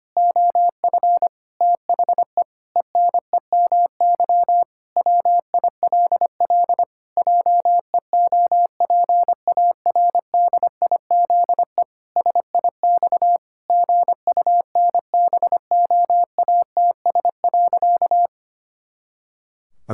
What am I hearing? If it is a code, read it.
Morse code, heard as a keyed tone, about 25 wpm, 700 Hz. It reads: OF THE ENEMY WILL JEOPARDIZE SIX GUNBOATS.
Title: Pangram Challenge v2 - 25wpm